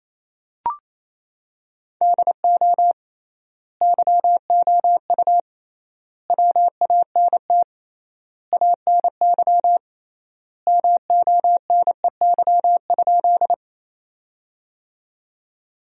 Morse code, heard as a keyed tone, about 27 wpm, 700 Hz. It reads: DO YOU WANT ANY MONEY?
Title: 3x Word-Spacing - Sentences from Top 400 Words - 28wpm